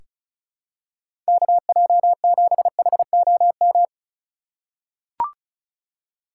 Morse code, heard as a keyed tone, about 36 wpm, 700 Hz. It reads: KJ7HOM
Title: USA Call Signs - 2x Character Spacing - 35wpm